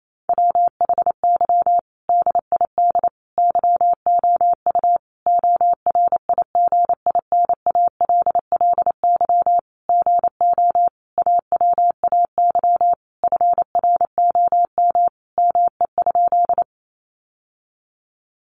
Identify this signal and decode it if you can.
Morse code, heard as a keyed tone, about 28 wpm, 700 Hz. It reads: WHY DID YOU ORIGINALLY GO AWAY FROM ME?
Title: Sentences from Top 1500 Words 28wpm